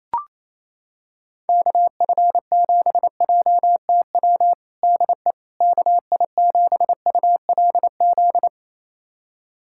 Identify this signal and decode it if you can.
Morse code, heard as a keyed tone, about 28 wpm, 700 Hz. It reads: KF7JTW DE KI7ULZ
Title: USA Callsign DE Callsign 28wpm